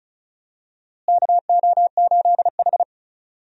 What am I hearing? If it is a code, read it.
Morse code, heard as a keyed tone, about 34 wpm, 700 Hz. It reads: KO8H
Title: USA Call Signs - 4 Characters - Repeated 2x - 35wpm